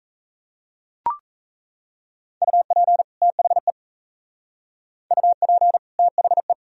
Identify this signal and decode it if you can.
Morse code, heard as a keyed tone, about 35 wpm, 700 Hz. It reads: UP THE UP THE
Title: Speed-Racing - Sets of 2 Words 28wpm